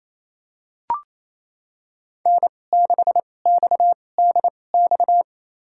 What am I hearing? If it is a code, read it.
Morse code, heard as a keyed tone, about 28 wpm, spaced as for 20 wpm, 700 Hz. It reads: N6XDX
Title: USA Call Signs - 2x Character Spacing - 28wpm